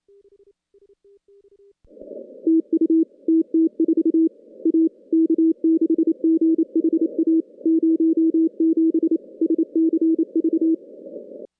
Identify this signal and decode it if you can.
Morse code, heard as a keyed tone, about 27 wpm, 320 Hz. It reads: TU TT4 A K6GHA 07 SCV
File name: contests / CW-SS-k5tr- -2200